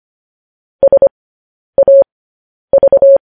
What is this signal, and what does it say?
Morse code, heard as a keyed tone, about 25 wpm, spaced as for 10 wpm, 550 Hz. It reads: SAV
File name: podcast / News headlines-1x-25x10